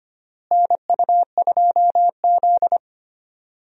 Morse code, heard as a keyed tone, about 25 wpm, 700 Hz. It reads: NU2Z
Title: USA Call Signs - 4 Characters - Repeated 3x - 25wpm